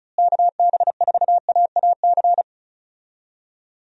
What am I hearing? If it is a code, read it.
Morse code, heard as a keyed tone, about 34 wpm, 700 Hz. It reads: KB4AAC
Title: USA Call Signs - 6 Characters - Repeated 2x - 35wpm